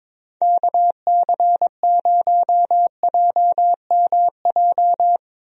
Morse code, heard as a keyed tone, about 22 wpm, 700 Hz. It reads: KC0JMJ
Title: USA Call Signs - 6 Characters - Repeated 3x - 22wpm